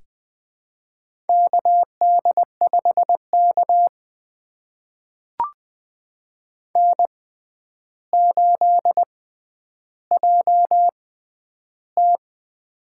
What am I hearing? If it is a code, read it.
Morse code, heard as a keyed tone, about 20 wpm, spaced as for 10 wpm, 700 Hz. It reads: KD5K N8JT
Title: USA Call Signs - 4 Characters - 6x Character Spacing - 20wpm